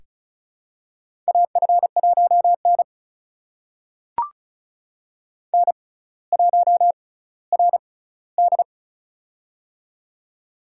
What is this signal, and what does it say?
Morse code, heard as a keyed tone, about 35 wpm, spaced as for 18 wpm, 700 Hz. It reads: AF1N N1RD
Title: USA Call Signs - 4 Characters - 6x Character Spacing - 35wpm